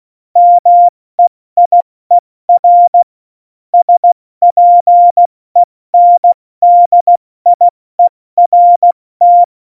Morse code, heard as a keyed tone, about 16 wpm, 700 Hz. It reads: MEIER SPENDIERT
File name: Morsekurs / Witze / Witz 03. November 2020, 70 F80